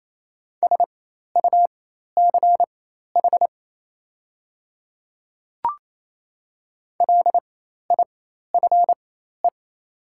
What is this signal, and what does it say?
Morse code, heard as a keyed tone, about 27 wpm, spaced as for 13 wpm, 700 Hz. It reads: SUCH LIFE
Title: 4x Character Spacing - Top 300 Words - 28wpm